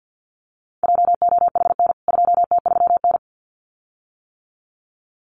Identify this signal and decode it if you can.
Morse code, heard as a keyed tone, about 54 wpm, 700 Hz. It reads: JOHN 2T3N
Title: CWT Exchange 50wpm